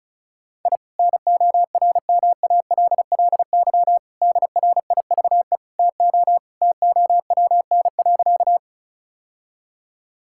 Morse code, heard as a keyed tone, about 34 wpm, 700 Hz. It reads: I NORMALLY DRIVE TO TOWN.
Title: Sentences from Top 1000 Words - Review Part 6 - 35wpm